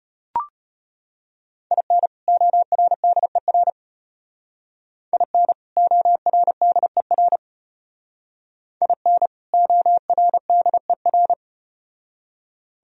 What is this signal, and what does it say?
Morse code, heard as a keyed tone, about 34 wpm, 700 Hz. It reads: IN ORDER IN ORDER IN ORDER